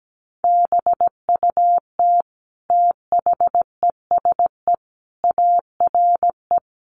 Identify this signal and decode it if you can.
Morse code, heard as a keyed tone, about 17 wpm, 700 Hz. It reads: BUT THESE ARE
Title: Sentences from Top 1300 Words 17wpm